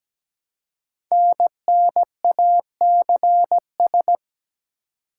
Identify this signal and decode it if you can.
Morse code, heard as a keyed tone, about 17 wpm, 700 Hz. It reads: NNACS
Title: Code Groups - 5 Letters - 17wpm